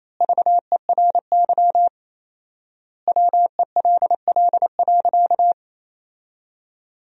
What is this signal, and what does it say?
Morse code, heard as a keyed tone, about 28 wpm, 700 Hz. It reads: VERY WELL.
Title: 4x Word-Spacing - Sentences from Top 500 Words - 28wpm